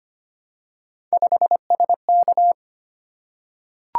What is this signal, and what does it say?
Morse code, heard as a keyed tone, about 25 wpm, 700 Hz. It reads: 5SK
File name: Three Letter-Number Combinations - 25wpm